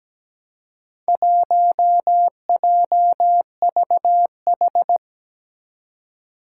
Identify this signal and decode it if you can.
Morse code, heard as a keyed tone, about 17 wpm, 700 Hz. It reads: 1JVH
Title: Four Challenging Letter-Number Combinations 17wpm